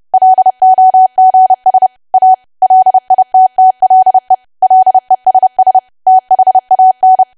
Morse code, heard as a keyed tone, about 30 wpm, 750 Hz. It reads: LOGS A LITTLE LESS THAN
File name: cwa / W1AW-Level3 / 30WPM